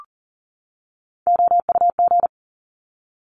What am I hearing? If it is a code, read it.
Morse code, heard as a keyed tone, about 37 wpm, 700 Hz. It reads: OUG
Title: Top Three Letter Combinations in Words 40wpm